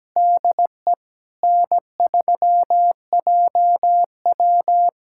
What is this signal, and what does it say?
Morse code, heard as a keyed tone, about 17 wpm, 700 Hz. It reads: DE N3JW
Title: USA Callsign DE Callsign - 4 Characters - 17wpm